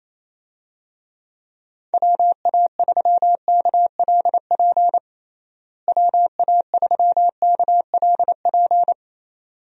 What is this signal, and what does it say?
Morse code, heard as a keyed tone, about 28 wpm, 700 Hz. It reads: WA3KLP WA3KLP
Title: USA Call Signs - 6 Characters - Repeated 2x - 28wpm